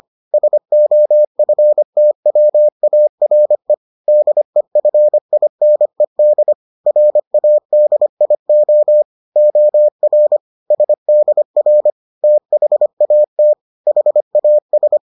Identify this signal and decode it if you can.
Morse code, heard as a keyed tone, about 25 wpm, 600 Hz. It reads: SOFTWARE DEFINED RADIO OR SDR THAT HAS